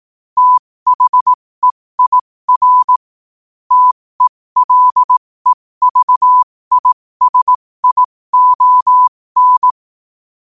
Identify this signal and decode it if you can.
Morse code, heard as a keyed tone, about 18 wpm, spaced as for 15 wpm, 1000 Hz. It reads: THEIR TELEVISION